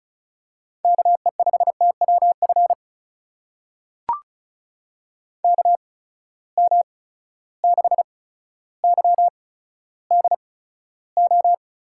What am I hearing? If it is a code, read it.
Morse code, heard as a keyed tone, about 36 wpm, spaced as for 15 wpm, 700 Hz. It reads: KE5TWF KM6YDO
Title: USA Call Signs - 6 Characters - 8x Character Spacing - 35wpm